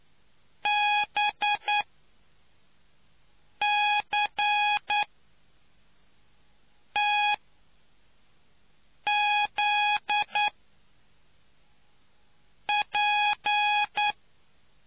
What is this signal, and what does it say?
Morse code, heard as a keyed tone, about 9 wpm, spaced as for 4 wpm, 830 Hz. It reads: BCTZP